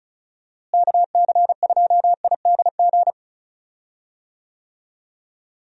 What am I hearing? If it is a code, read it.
Morse code, heard as a keyed tone, about 36 wpm, 700 Hz. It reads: KC2IDG
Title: USA Call Signs - 6 Characters - Repeated 2x - 35wpm